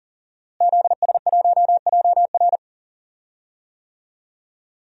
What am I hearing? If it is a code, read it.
Morse code, heard as a keyed tone, about 37 wpm, 700 Hz. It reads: ZS1JR
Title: DXCC-100 Call Signs - 5 Characters - Repeated 3x - 40wpm